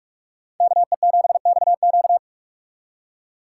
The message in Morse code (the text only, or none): KE7XQ